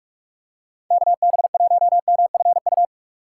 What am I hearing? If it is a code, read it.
Morse code, heard as a keyed tone, about 47 wpm, 700 Hz. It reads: KB1MFU
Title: USA Call Signs - 6x Character Spacing - 45wpm